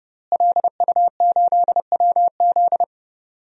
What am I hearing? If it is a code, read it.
Morse code, heard as a keyed tone, about 30 wpm, 700 Hz. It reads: LU8WZ
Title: DXCC-100 Call Signs - 5 Characters - Repeated 3x - 30wpm